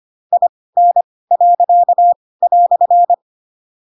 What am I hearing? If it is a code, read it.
Morse code, heard as a keyed tone, about 25 wpm, spaced as for 18 wpm, 700 Hz. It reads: IN."
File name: Pinocchio / Pinocchio ch7 12 25 ik1hge